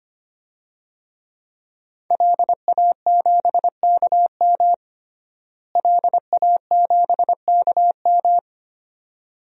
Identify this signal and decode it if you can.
Morse code, heard as a keyed tone, about 25 wpm, 700 Hz. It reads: LA7KM LA7KM
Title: DXCC-100 Call Signs - 5 Characters - Repeated 2x - 25wpm